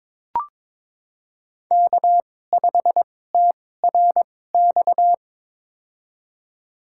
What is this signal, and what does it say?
Morse code, heard as a keyed tone, about 22 wpm, spaced as for 16 wpm, 700 Hz. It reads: K5TRX
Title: USA Call Signs - 5 Characters - 2x Character Spacing - 22wpm